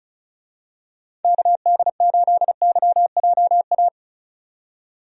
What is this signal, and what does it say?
Morse code, heard as a keyed tone, about 36 wpm, 700 Hz. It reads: KD8YJA
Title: USA Call Signs - 6 Characters - 35wpm